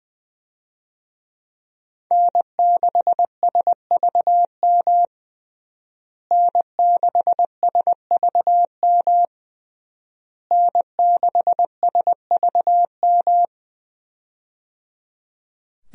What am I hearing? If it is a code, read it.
Morse code, heard as a keyed tone, about 20 wpm, 700 Hz. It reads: N6SVM N6SVM N6SVM
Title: USA Call Signs - 5 Characters - Repeated 3x - 20wpm